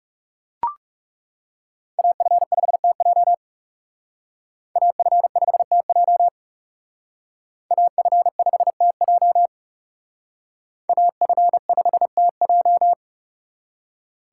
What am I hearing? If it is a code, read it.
Morse code, heard as a keyed tone, about 39 wpm, 700 Hz. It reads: AF5TJ AF5TJ AF5TJ AF5TJ